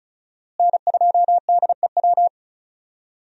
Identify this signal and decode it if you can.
Morse code, heard as a keyed tone, about 36 wpm, 700 Hz. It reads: N2DEW
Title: USA Call Signs - 5 Characters - Repeated 3x - 35wpm